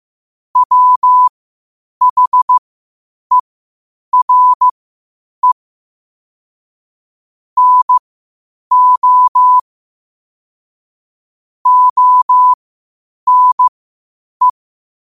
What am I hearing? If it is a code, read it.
Morse code, heard as a keyed tone, about 15 wpm, spaced as for 8 wpm, 1000 Hz. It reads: WHERE NO ONE
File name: qotdfiles / 8wpm